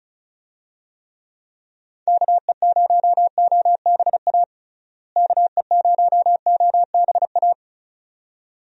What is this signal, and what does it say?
Morse code, heard as a keyed tone, about 36 wpm, 700 Hz. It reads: KE0OBA KE0OBA
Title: USA Call Signs - 6 Characters - Repeated 2x - 35wpm